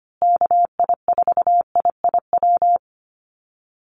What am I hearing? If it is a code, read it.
Morse code, heard as a keyed tone, about 25 wpm, 700 Hz. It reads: KI4IIW
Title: USA Call Signs Repeated 3x 25wpm